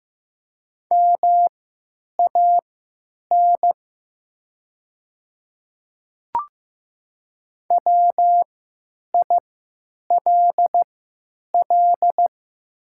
Morse code, heard as a keyed tone, about 15 wpm, spaced as for 9 wpm, 700 Hz. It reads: MAN WILL
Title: 3x Character Spacing - Top 300 Words - 15wpm